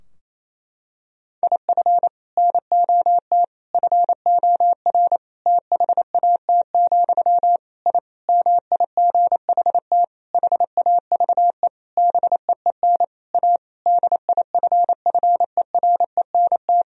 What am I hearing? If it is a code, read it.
Morse code, heard as a keyed tone, about 28 wpm, 700 Hz. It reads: IF NOT FOR THAT, I MIGHT HAVE BEEN A DIFFERENT
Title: Sentences from Top 1000 Words - Review Part 3 - 28wpm